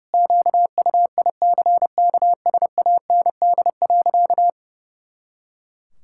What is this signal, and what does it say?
Morse code, heard as a keyed tone, about 30 wpm, 700 Hz. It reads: QUICKSAND.
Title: Pangram Challenge v2 - 30wpm